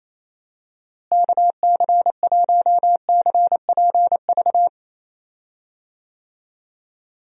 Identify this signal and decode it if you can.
Morse code, heard as a keyed tone, about 28 wpm, 700 Hz. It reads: KC1CPV
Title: USA Call Signs - 6 Characters - Repeated 3x - 28wpm